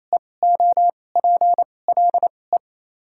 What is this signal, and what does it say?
Morse code, heard as a keyed tone, about 28 wpm, spaced as for 20 wpm, 700 Hz. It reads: EOPLE
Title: 2x Character Spacing - Top 200 Words - 28wpm